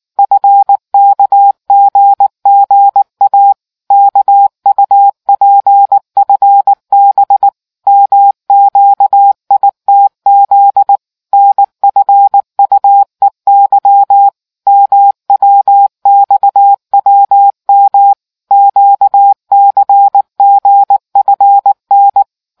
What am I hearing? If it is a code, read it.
Morse code, heard as a keyed tone, about 19 wpm, 800 Hz. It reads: FKGGA KUPFB MQITZ NFUEY MWXWM QCGFN